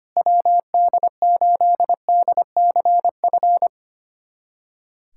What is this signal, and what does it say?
Morse code, heard as a keyed tone, about 25 wpm, 700 Hz. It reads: WD8DCF
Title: USA Callsign DE Callsign - 6 Characters - 25wpm